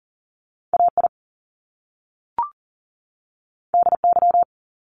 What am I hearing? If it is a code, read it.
Morse code, heard as a keyed tone, about 40 wpm, 700 Hz. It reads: AI DY
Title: Top Two Letter Combinations in Words 40wpm